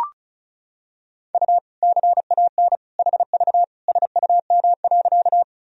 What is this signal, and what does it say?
Morse code, heard as a keyed tone, about 36 wpm, 700 Hz. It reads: U CAN HV SUM.
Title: Sentences from Top 100 Words and Abbreviations 35wpm